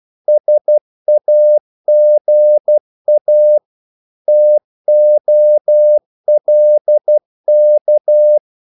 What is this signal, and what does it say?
Morse code, heard as a keyed tone, about 12 wpm, 600 Hz. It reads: SAGA TOLK
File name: Korte ord DK 12wpm